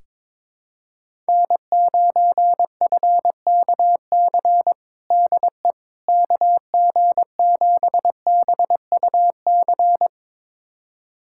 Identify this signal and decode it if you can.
Morse code, heard as a keyed tone, about 22 wpm, 700 Hz. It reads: N9FKC DE KG7BUC
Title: USA Callsign DE Callsign 22wpm